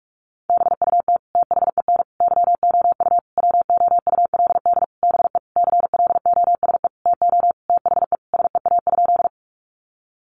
Morse code, heard as a keyed tone, about 44 wpm, 700 Hz. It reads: BUT THEN YOU WOULD BE CLOSE TO THE SEA?
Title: Sentences from Top 300 Words 45wpm